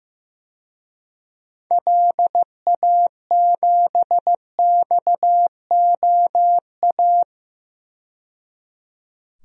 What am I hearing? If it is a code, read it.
Morse code, heard as a keyed tone, about 16 wpm, 700 Hz. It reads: LA7XOA
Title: DXCC-100 Call Signs - 6 Characters - 15wpm